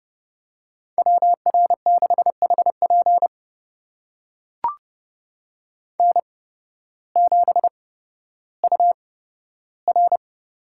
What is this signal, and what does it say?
Morse code, heard as a keyed tone, about 30 wpm, spaced as for 14 wpm, 700 Hz. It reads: WR6HP N7UR
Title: USA Call Signs - 5 Characters - 8x Character Spacing - 30wpm